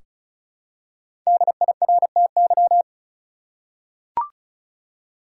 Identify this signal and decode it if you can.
Morse code, heard as a keyed tone, about 34 wpm, 700 Hz. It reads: DIRTY
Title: Five Letter Words - 35wpm